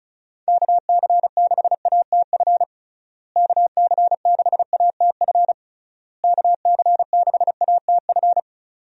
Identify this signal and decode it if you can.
Morse code, heard as a keyed tone, about 36 wpm, 700 Hz. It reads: KC6ATF KC6ATF KC6ATF
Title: USA Call Signs - 6 Characters - Repeated 3x - 35wpm